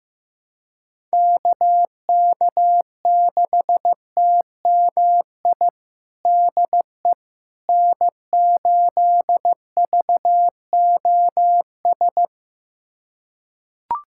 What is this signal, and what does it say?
Morse code, heard as a keyed tone, about 15 wpm, 700 Hz. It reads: KK6TMI DE N8VOS
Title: USA Callsign DE Callsign 15wpm